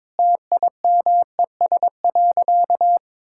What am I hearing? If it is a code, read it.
Morse code, heard as a keyed tone, about 21 wpm, 700 Hz. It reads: TIMES.
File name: practice / 22wpm / Sentences with 3-Word Phrases - 22wpm